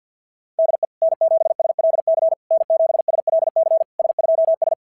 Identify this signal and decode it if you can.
Morse code, heard as a keyed tone, about 46 wpm, 660 Hz. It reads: DE N7SLC N7SLC S2S
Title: SOTA QSO Practice 50wpm